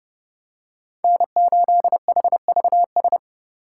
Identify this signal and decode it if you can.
Morse code, heard as a keyed tone, about 31 wpm, 700 Hz. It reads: N8HVS
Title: USA Call Signs - 5 Characters - 4x Character Spacing - 30wpm